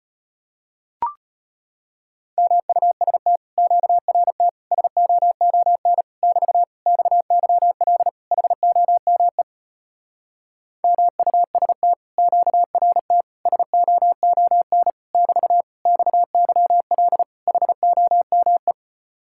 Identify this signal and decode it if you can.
Morse code, heard as a keyed tone, about 33 wpm, 700 Hz. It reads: MUST QRT SOON <BT> XYL HOME MUST QRT SOON <BT> XYL HOME